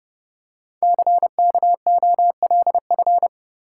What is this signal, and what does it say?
Morse code, heard as a keyed tone, about 30 wpm, 700 Hz. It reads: CKOLF
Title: Code Groups - 5 Letters - 30wpm